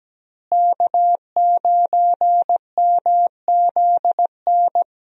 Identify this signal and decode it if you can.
Morse code, heard as a keyed tone, about 17 wpm, 700 Hz. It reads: K9MZN
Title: USA Call Signs - 5 Characters - Repeated 3x - 17wpm